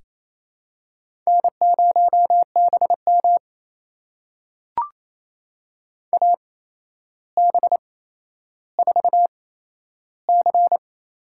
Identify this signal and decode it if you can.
Morse code, heard as a keyed tone, about 28 wpm, spaced as for 12 wpm, 700 Hz. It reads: N0BM AB4C
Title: USA Call Signs - 4 Characters - 8x Character Spacing - 28wpm